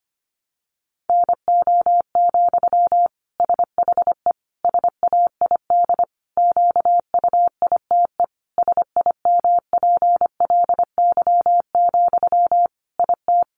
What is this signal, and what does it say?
Morse code, heard as a keyed tone, about 25 wpm, 700 Hz. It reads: NO, SHE SAID QUITE SIMPLY, IT
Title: Sentences from Top 800 Words 25wpm